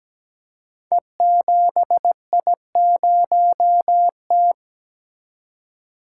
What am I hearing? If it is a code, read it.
Morse code, heard as a keyed tone, about 17 wpm, 700 Hz. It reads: E7I0T
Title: Code Groups - 5 Letter:Numbers - 17wpm